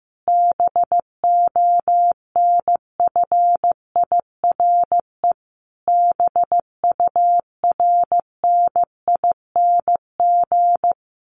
Morse code, heard as a keyed tone, about 15 wpm, 700 Hz. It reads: BONFIRE BURNING
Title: A Sound Of Thunder 15wpm